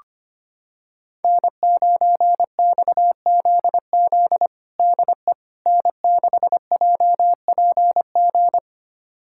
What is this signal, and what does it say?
Morse code, heard as a keyed tone, about 25 wpm, 700 Hz. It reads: N9XZZ DE N6JPG
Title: USA Callsign DE Callsign - 5 Characters - 25wpm